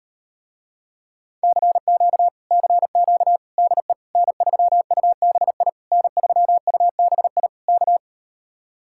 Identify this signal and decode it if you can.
Morse code, heard as a keyed tone, about 38 wpm, 700 Hz. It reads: CQ CQ DE N3UBI N3UBI K